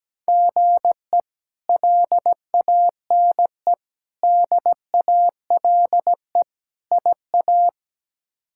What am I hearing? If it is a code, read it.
Morse code, heard as a keyed tone, about 17 wpm, 700 Hz. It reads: GE LANE DALE IA
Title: K1USN SST QSO Practice - 17wpm